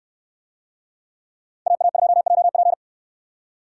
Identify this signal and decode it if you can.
Morse code, heard as a keyed tone, about 68 wpm, 700 Hz. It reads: AA2JP